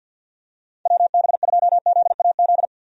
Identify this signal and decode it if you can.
Morse code, heard as a keyed tone, about 52 wpm, 700 Hz. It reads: WB2ZAZ
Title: USA Call Signs - 6 Characters - Repeated 3x - 50wpm